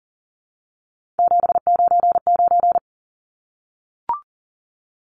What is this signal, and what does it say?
Morse code, heard as a keyed tone, about 39 wpm, 700 Hz. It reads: 799
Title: Numbers - 3-Digit 40wpm